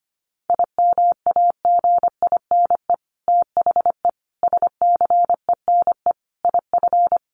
Morse code, heard as a keyed tone, about 25 wpm, 700 Hz. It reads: IMAGINE THE SCENE IF
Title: Sentences from Top 1400 Words 25wpm